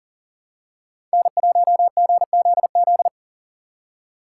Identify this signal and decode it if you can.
Morse code, heard as a keyed tone, about 38 wpm, 700 Hz. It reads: N1GZZ
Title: USA Call Signs - 5 Characters - 2x Character Spacing - 40wpm